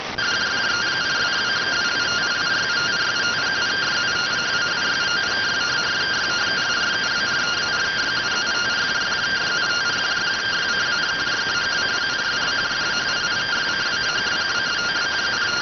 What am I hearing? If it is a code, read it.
Two-tone teleprinter signal, about 170 Hz shift.